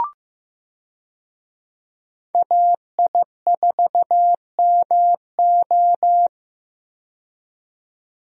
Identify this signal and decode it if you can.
Morse code, heard as a keyed tone, about 15 wpm, 700 Hz. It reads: AI4MO
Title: USA Call Signs - 5 Characters - 15wpm